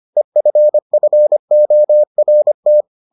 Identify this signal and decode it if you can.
Morse code, heard as a keyed tone, about 25 wpm, 600 Hz. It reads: EFFORT